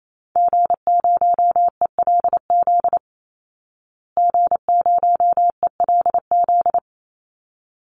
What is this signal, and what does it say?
Morse code, heard as a keyed tone, about 28 wpm, 700 Hz. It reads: G0ELZ G0ELZ
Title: Call Signs Repeated 3x - 28wpm